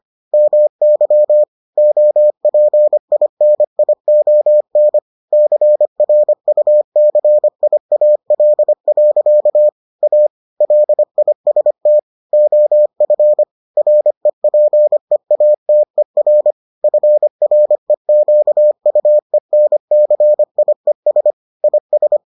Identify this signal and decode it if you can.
Morse code, heard as a keyed tone, about 25 wpm, 600 Hz. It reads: MY OPINION CRUCIAL. A LIST OF REPEATER FREQUENCIES IS